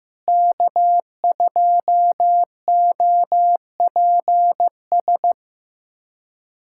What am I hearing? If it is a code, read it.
Morse code, heard as a keyed tone, about 16 wpm, 700 Hz. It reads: K2OPS
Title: USA Callsign DE Callsign - 5 Characters - 15wpm